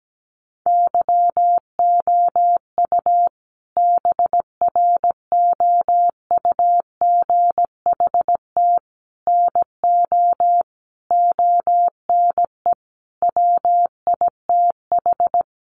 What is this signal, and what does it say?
Morse code, heard as a keyed tone, about 17 wpm, 700 Hz. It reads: YOU BROUGHT NO ONE WITH
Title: Sentences from Top 500 Words 17wpm